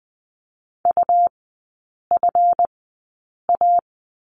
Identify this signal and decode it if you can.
Morse code, heard as a keyed tone, about 20 wpm, spaced as for 8 wpm, 700 Hz. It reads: UFA
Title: 138 Lesson 23e - Letter f - Three Characters 20wpm